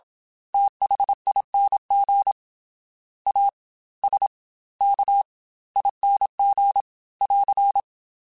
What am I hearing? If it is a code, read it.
Morse code, heard as a keyed tone, about 27 wpm, spaced as for 19 wpm, 800 Hz. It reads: THING ASKING <AR>
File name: Special INGs / Special INGs-mp3s / Fast ing @25